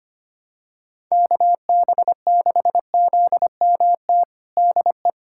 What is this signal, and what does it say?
Morse code, heard as a keyed tone, about 25 wpm, 700 Hz. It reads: KB6ZMT DE
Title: USA Callsign DE Callsign 25wpm